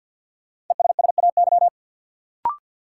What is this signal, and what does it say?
Morse code, heard as a keyed tone, about 52 wpm, 700 Hz. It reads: ESSAY
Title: Sentences from Top 1500 Words - New Word Drill Alt - 50wpm